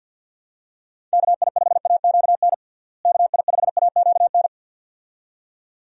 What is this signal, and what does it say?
Morse code, heard as a keyed tone, about 52 wpm, 700 Hz. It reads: KI5AQN KI5AQN
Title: USA Call Signs - 6 Characters - Repeated 2x - 50wpm